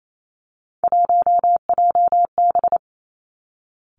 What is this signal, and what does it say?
Morse code, heard as a keyed tone, about 28 wpm, 700 Hz. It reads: 1JB
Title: Three Challenging Letter-Number Combinations 28wpm